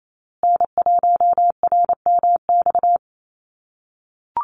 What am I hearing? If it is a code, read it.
Morse code, heard as a keyed tone, about 28 wpm, 700 Hz. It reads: N1RMX
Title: USA Call Signs Repeated 2x 28wpm